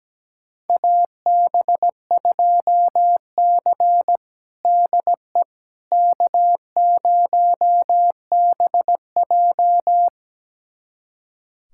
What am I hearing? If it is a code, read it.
Morse code, heard as a keyed tone, about 17 wpm, 700 Hz. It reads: AB2C DE K0BJ
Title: USA Callsign DE Callsign - 4 Characters - 17wpm